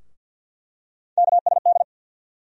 Morse code, heard as a keyed tone, about 47 wpm, 700 Hz. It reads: KSD
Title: USA Call Sign Postfixes - 3 Letters 50wpm